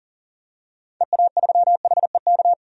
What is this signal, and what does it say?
Morse code, heard as a keyed tone, about 38 wpm, 700 Hz. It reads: EA3HEK
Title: DXCC-100 Call Signs - 6 Characters - 40wpm